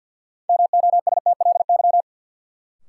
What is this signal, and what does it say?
Morse code, heard as a keyed tone, about 52 wpm, 700 Hz. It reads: MOSTLY
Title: Sentences from Top 1300 Words - New Word Drill - 50wpm